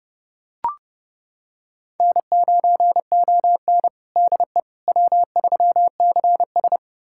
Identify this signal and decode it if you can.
Morse code, heard as a keyed tone, about 30 wpm, 700 Hz. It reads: N9ON DE W3CS
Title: USA Callsign DE Callsign - 4 Characters - 30wpm